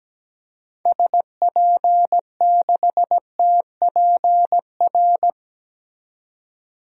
Morse code, heard as a keyed tone, about 17 wpm, 700 Hz. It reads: SP6TPR
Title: DXCC-100 Call Signs - 6 Characters - 17wpm